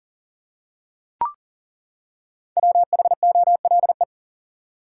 Morse code, heard as a keyed tone, about 40 wpm, 700 Hz. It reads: WHOLE
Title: Five Letter Words - 40wpm